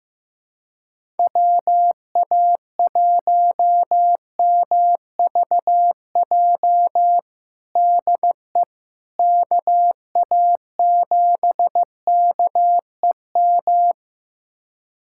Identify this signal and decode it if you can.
Morse code, heard as a keyed tone, about 15 wpm, 700 Hz. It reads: WA1MVJ DE KA7KEM
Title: USA Callsign DE Callsign - 6 Characters - 15wpm